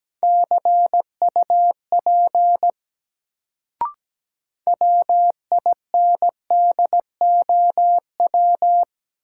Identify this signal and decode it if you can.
Morse code, heard as a keyed tone, about 17 wpm, 700 Hz. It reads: CUP WINDOW
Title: Sentences from Top 1600 Words - New Word Drill - 17wpm